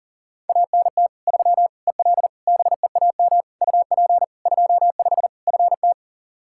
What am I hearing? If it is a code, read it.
Morse code, heard as a keyed tone, about 40 wpm, 700 Hz. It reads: ANT 3 EL BEAM UP 25 FT
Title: Practice QSO - Exchange 2 - Info - 40wpm